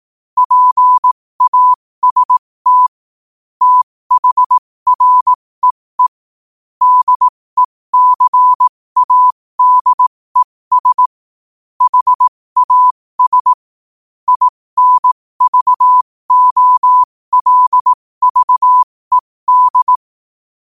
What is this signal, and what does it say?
Morse code, heard as a keyed tone, about 18 wpm, spaced as for 15 wpm, 1000 Hz. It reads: PAST THREE DECADES HAS INVOLVED
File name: qotdfiles / 15wpm